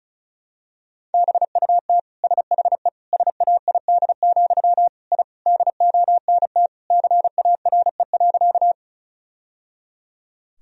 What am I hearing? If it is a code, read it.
Morse code, heard as a keyed tone, about 34 wpm, 700 Hz. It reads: BUT SHE SAID, I DONT CARE.
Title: Sentences from Top 1000 Words - Review Part 1 - 35wpm